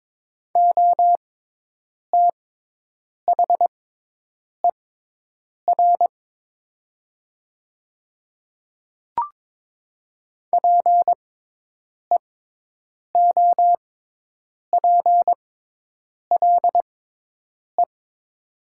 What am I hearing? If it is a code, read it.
Morse code, heard as a keyed tone, about 23 wpm, spaced as for 8 wpm, 700 Hz. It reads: OTHER PEOPLE
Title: 6x Character Spacing - Top 100 Words - 22wpm